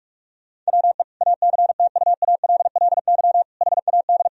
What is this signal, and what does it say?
Morse code, heard as a keyed tone, about 46 wpm, 700 Hz. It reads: WE ACTUALLY HAD
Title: Sentences from Top 1000 Words - Review Part 4 - 45wpm